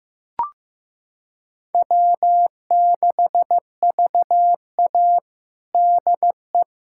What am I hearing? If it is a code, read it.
Morse code, heard as a keyed tone, about 15 wpm, 700 Hz. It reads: W6VA DE
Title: USA Callsign DE Callsign - 4 Characters - 15wpm